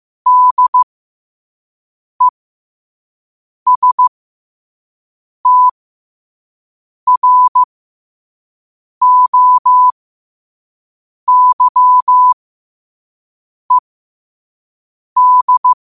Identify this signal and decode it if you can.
Morse code, heard as a keyed tone, about 15 wpm, spaced as for 5 wpm, 1000 Hz. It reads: DESTROYED
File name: qotdfiles / 5wpm